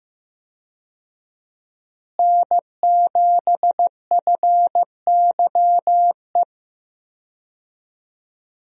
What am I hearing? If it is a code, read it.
Morse code, heard as a keyed tone, about 15 wpm, 700 Hz. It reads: N7FYE
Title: USA Call Signs - 5 Characters - Repeated 3x - 15wpm